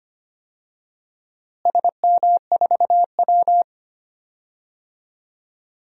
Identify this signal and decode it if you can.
Morse code, heard as a keyed tone, about 25 wpm, 700 Hz. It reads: SM4W
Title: DXCC-100 Call Signs - 4 Characters - Repeated 2x - 25wpm